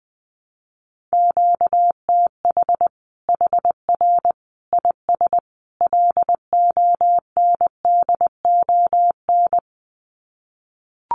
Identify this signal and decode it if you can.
Morse code, heard as a keyed tone, about 20 wpm, 700 Hz. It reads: QTH HR IS LONDON
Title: Common QSO Phrases 20wpm